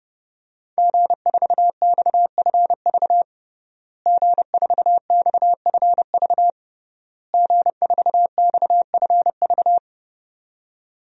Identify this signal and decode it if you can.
Morse code, heard as a keyed tone, about 30 wpm, 700 Hz. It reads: G4XFV G4XFV G4XFV
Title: DXCC-100 Call Signs - 5 Characters - Repeated 3x - 30wpm